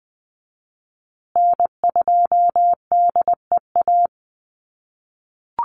Morse code, heard as a keyed tone, about 20 wpm, 700 Hz. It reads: N2DEA
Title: USA Call Signs Repeated 3x 20wpm